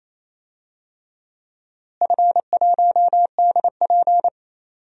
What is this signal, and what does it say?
Morse code, heard as a keyed tone, about 27 wpm, 700 Hz. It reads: F1DP
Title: DXCC-100 Call Signs - 4 Characters - Repeated 3x - 28wpm